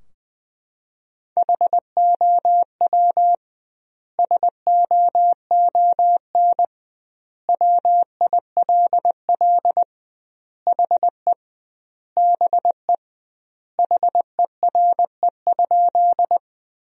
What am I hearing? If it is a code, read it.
Morse code, heard as a keyed tone, about 20 wpm, 700 Hz. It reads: HOW SOON WILL HE BE HERE?
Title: 2x Word-Spacing - Sentences from Top 400 Words - 20wpm